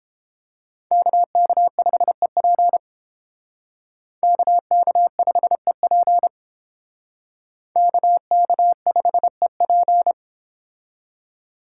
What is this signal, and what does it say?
Morse code, heard as a keyed tone, about 29 wpm, 700 Hz. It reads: KK5EP KK5EP KK5EP